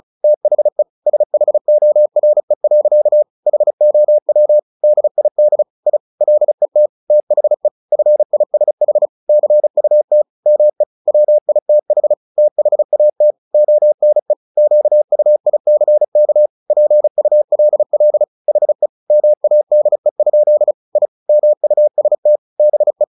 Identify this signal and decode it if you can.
Morse code, heard as a keyed tone, about 36 wpm, 600 Hz. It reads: THE SHORE. HOW DID I LET THE FISH CUT ME WITH THAT ONE QUICK PULL HE MADE? I MUST BE
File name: the old man and the sea / the old man and the sea 1 35wpm